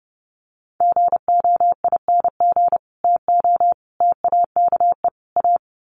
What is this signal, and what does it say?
Morse code, heard as a keyed tone, about 30 wpm, 700 Hz. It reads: GOING TO TAKE A
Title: Sets of 4 Words 30wpm